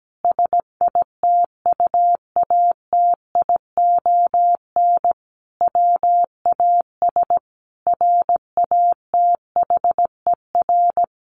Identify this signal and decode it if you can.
Morse code, heard as a keyed tone, about 17 wpm, 700 Hz. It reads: SITUATION WAS RATHER
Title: Sentences from Top 1300 Words 17wpm